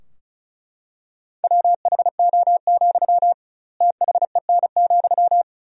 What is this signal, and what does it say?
Morse code, heard as a keyed tone, about 35 wpm, 700 Hz. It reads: WHO, THEN,
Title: 2x Word-Spacing - Sentences from Top 100 Words - 35wpm